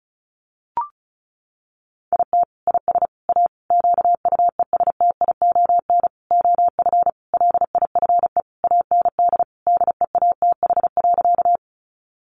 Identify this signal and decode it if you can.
Morse code, heard as a keyed tone, about 34 wpm, 700 Hz. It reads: IT IS A QUESTION OF LIFE AND DEATH.
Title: Sentences from Top 1100 Words 35wpm